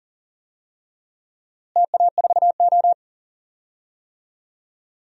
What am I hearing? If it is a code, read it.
Morse code, heard as a keyed tone, about 40 wpm, 700 Hz. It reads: TA4O
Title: DXCC-100 Call Signs - 4 Characters - Repeated 3x - 40wpm